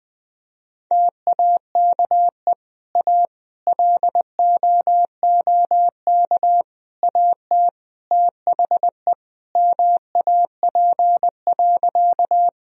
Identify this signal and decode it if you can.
Morse code, heard as a keyed tone, about 20 wpm, 700 Hz. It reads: TAKE A LOOK AT THE MAP.
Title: Sentences from Top 1000 Words - Review Part 2 - 20wpm